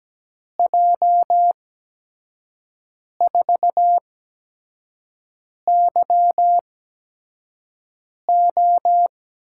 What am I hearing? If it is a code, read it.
Morse code, heard as a keyed tone, about 17 wpm, spaced as for 5 wpm, 700 Hz. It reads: J4YO
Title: USA Call Signs - 8x Character Spacing - 17wpm